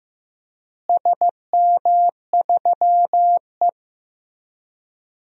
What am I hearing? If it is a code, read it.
Morse code, heard as a keyed tone, about 16 wpm, 700 Hz. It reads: SM3E
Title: DXCC-100 Call Signs - 4 Characters - Repeated 3x - 15wpm